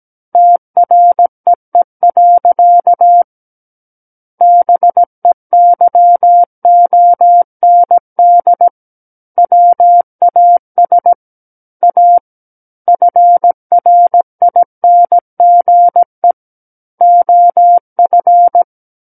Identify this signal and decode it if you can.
Morse code, heard as a keyed tone, about 17 wpm, 700 Hz. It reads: TREE. BEYOND WAS A FRINGE OF
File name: Book One / 08-Chapter 8 - 17 WPM